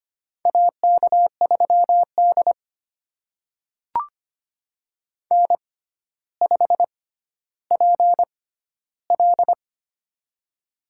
Morse code, heard as a keyed tone, about 25 wpm, spaced as for 13 wpm, 700 Hz. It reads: AK3D N5PL
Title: USA Call Signs - 4 Characters - 6x Character Spacing - 25wpm